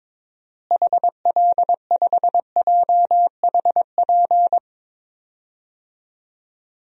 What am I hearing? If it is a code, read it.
Morse code, heard as a keyed tone, about 21 wpm, 700 Hz. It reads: HL5JHP
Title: DXCC-100 Call Signs - 6 Characters - Repeated 2x - 22wpm